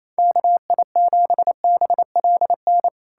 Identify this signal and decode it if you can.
Morse code, heard as a keyed tone, about 28 wpm, 700 Hz. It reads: KI7BLN
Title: USA Callsign DE Callsign 28wpm